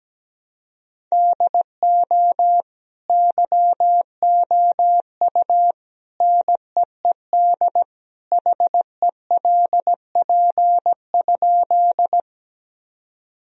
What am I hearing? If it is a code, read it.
Morse code, heard as a keyed tone, about 17 wpm, 700 Hz. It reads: DO YOU NEED HELP?
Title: Sentences from Top 1000 Words - Review Part 1 - 17wpm